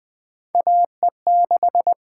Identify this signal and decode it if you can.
Morse code, heard as a keyed tone, about 20 wpm, 700 Hz. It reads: AE6
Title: USA Call Signs - 5 Characters - 8x Character Spacing - 20wpm